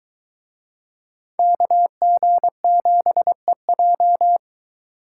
Morse code, heard as a keyed tone, about 22 wpm, 700 Hz. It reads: KG7EJ